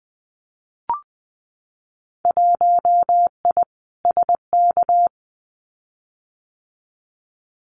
Morse code, heard as a keyed tone, about 20 wpm, 700 Hz. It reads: 1I SK
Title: Winter Field Day Exchanges Encore 20wpm